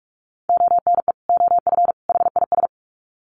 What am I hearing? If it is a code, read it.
Morse code, heard as a keyed tone, about 44 wpm, 700 Hz. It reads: ONE OF HIS
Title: Sets of 3 Words 45wpm